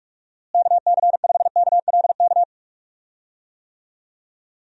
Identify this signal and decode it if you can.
Morse code, heard as a keyed tone, about 45 wpm, 700 Hz. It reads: KC5KLK
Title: USA Call Signs - 6 Characters - Repeated 2x - 45wpm